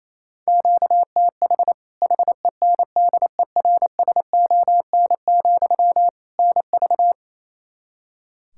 Morse code, heard as a keyed tone, about 28 wpm, 700 Hz. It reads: QTH HENDERSON, NV
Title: Practice QSO - Exchange 1 - Beginning - 28wpm